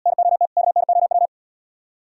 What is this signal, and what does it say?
Morse code, heard as a keyed tone, about 72 wpm, 700 Hz. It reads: NOT KNOW